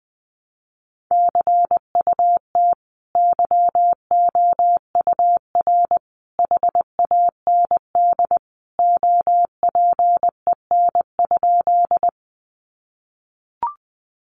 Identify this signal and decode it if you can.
Morse code, heard as a keyed tone, about 20 wpm, 700 Hz. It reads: CUT YOUR HAND OPEN?
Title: Sentences from Top 300 Words 20wpm